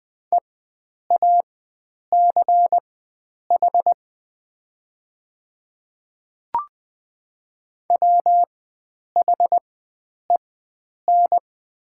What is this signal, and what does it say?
Morse code, heard as a keyed tone, about 20 wpm, spaced as for 9 wpm, 700 Hz. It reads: EACH WHEN
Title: 4x Character Spacing - Top 100 Words - 20wpm